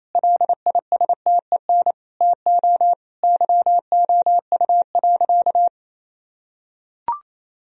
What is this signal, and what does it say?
Morse code, heard as a keyed tone, about 28 wpm, 700 Hz. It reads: LISTEN TO YOU.
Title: Sentences from Top 1000 Words - Review Part 3 - 28wpm